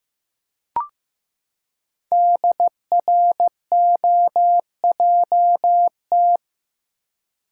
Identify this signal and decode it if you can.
Morse code, heard as a keyed tone, about 15 wpm, 700 Hz. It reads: DROJT
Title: Code Groups - 5 Letters - 15wpm